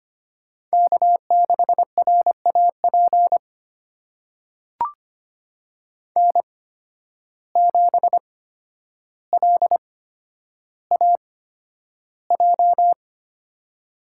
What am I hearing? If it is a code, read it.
Morse code, heard as a keyed tone, about 25 wpm, spaced as for 11 wpm, 700 Hz. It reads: K6RAP N7LAJ
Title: USA Call Signs - 5 Characters - 8x Character Spacing - 25wpm